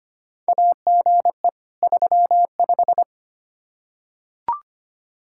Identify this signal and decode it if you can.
Morse code, heard as a keyed tone, about 25 wpm, 700 Hz. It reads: AGE 35
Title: Practice QSO - Exchange 3 - More Info - 25wpm